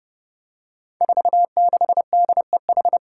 Morse code, heard as a keyed tone, about 30 wpm, 700 Hz. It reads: 46DEH
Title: Five Challenging Letter-Number Combinations 30wpm